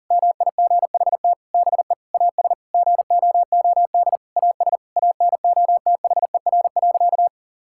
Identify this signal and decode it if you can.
Morse code, heard as a keyed tone, about 40 wpm, 700 Hz. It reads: MIGHT BE AS GOOD AS ANOTHER.
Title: Sentences from Top 1600 Words 40wpm